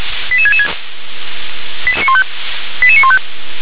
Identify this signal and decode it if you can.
Five-tone selective call (ZVEI), heard as quick runs of five tones, tones of about 80 ms. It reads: 88666 88915